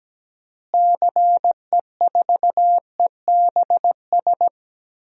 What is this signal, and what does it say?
Morse code, heard as a keyed tone, about 17 wpm, 700 Hz. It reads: CE4EBS
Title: DXCC-100 Call Signs - 6 Characters - Repeated 2x - 17wpm